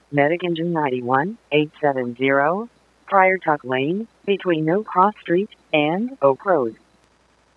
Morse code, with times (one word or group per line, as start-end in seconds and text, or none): none